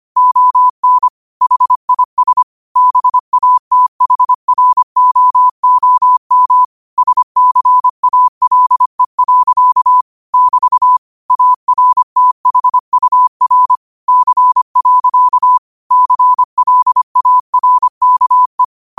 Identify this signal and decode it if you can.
Morse code, heard as a keyed tone, about 25 wpm, 1000 Hz. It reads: ON HIS BATHROOM SCALE. <BT> ARTHUR C. CLARKE